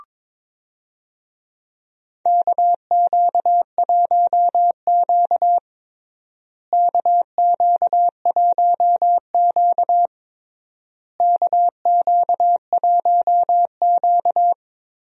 Morse code, heard as a keyed tone, about 22 wpm, 700 Hz. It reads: KQ1Q KQ1Q KQ1Q
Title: USA Call Signs - 4 Characters - Repeated 3x - 22wpm